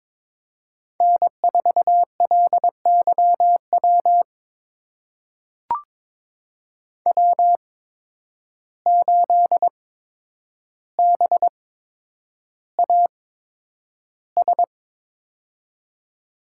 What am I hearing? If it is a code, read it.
Morse code, heard as a keyed tone, about 23 wpm, spaced as for 10 wpm, 700 Hz. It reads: N4LYW W8BAS
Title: USA Call Signs - 5 Characters - 8x Character Spacing - 22wpm